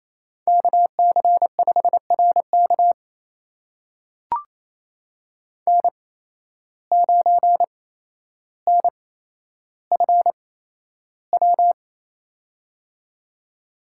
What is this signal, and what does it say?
Morse code, heard as a keyed tone, about 28 wpm, spaced as for 12 wpm, 700 Hz. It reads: KC5RK N9NFW
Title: USA Call Signs - 5 Characters - 8x Character Spacing - 28wpm